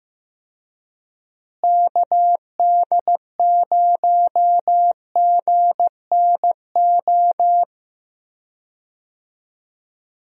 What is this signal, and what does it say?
Morse code, heard as a keyed tone, about 15 wpm, 700 Hz. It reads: KD0GNO
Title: USA Call Signs - 6 Characters - Repeated 2x - 15wpm